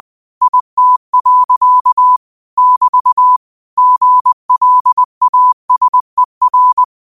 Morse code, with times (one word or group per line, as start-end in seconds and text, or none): 0.4-2.2 IT.
2.6-3.4 <BT>
3.8-6.8 GLASER